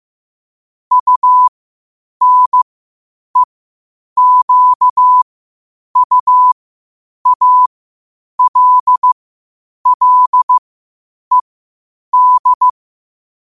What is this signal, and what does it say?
Morse code, heard as a keyed tone, about 15 wpm, spaced as for 8 wpm, 1000 Hz. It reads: UNEQUALLED
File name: qotdfiles / 8wpm